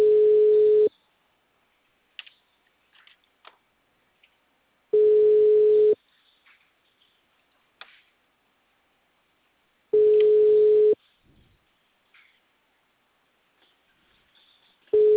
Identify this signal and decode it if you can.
Ringback tone (Europe, 425 Hz).
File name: out- -2003- - - .45032